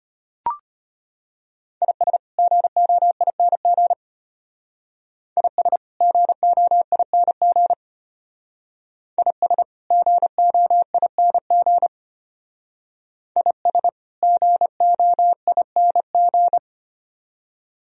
Morse code, heard as a keyed tone, about 31 wpm, 700 Hz. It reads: IS GOING IS GOING IS GOING IS GOING